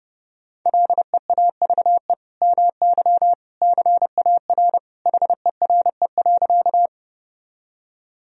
Morse code, heard as a keyed tone, about 30 wpm, 700 Hz. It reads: LEAVE MY CAR HERE.
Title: Sentences from Top 1000 Words - Review Part 1 - 30wpm